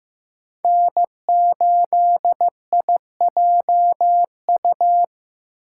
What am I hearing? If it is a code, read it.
Morse code, heard as a keyed tone, about 18 wpm, 700 Hz. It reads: N8IJU
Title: Speed Racing - USA 1x3 Call Signs 15wpm